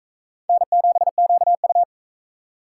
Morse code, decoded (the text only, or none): N7QU